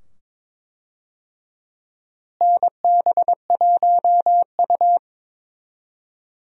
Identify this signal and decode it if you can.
Morse code, heard as a keyed tone, about 21 wpm, 700 Hz. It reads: NB1U
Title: USA Call Signs - 4 Characters - Repeated 2x - 22wpm